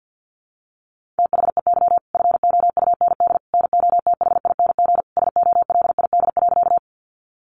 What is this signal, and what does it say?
Morse code, heard as a keyed tone, about 50 wpm, 700 Hz. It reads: THEY FOUND NOTHING SOLID.
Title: Sentences from Top 1500 Words 50wpm